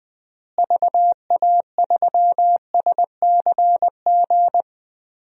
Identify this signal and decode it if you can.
Morse code, heard as a keyed tone, about 20 wpm, 700 Hz. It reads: VA3SCG
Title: DXCC-100 Call Signs - 6 Characters - Repeated 2x - 20wpm